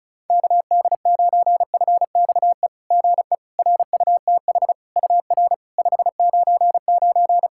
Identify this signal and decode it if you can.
Morse code, heard as a keyed tone, about 36 wpm, 690 Hz. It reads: KD9FXE GE RUTH UR 599
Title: SOTA QSO Practice 35wpm